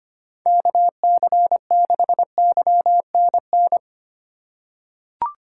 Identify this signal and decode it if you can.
Morse code, heard as a keyed tone, about 25 wpm, 700 Hz. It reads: KC6YNN
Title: USA Call Signs - 6 Characters - 2x Character Spacing - 25wpm